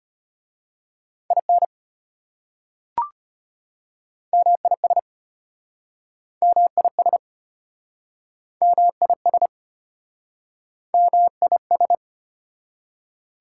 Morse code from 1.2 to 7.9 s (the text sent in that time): IN MIS MIS